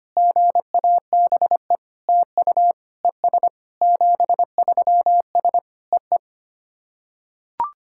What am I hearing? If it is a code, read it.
Morse code, heard as a keyed tone, about 25 wpm, 700 Hz. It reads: GABE TU ES 73S EE
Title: SOTA QSO Practice 25wpm